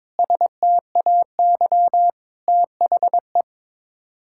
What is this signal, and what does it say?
Morse code, heard as a keyed tone, about 22 wpm, 700 Hz. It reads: STAY THE
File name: Sets of 9 Words - 22wpm